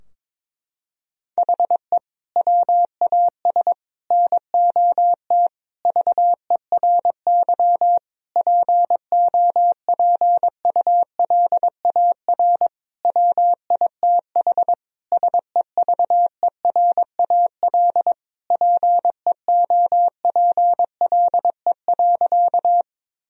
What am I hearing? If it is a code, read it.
Morse code, heard as a keyed tone, about 22 wpm, 700 Hz. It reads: HE WAS NOT VERY POPULAR WITH SEVERAL PEOPLE.
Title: Sentences from Top 1000 Words - Review Part 3 - 22wpm